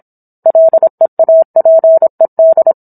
Morse code, heard as a keyed tone, about 26 wpm, 650 Hz. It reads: LEAPED